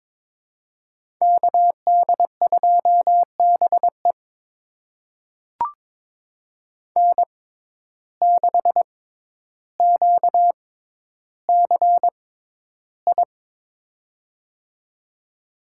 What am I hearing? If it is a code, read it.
Morse code, heard as a keyed tone, about 22 wpm, spaced as for 11 wpm, 700 Hz. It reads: KD2BE N6QCI
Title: USA Call Signs - 5 Characters - 6x Character Spacing - 22wpm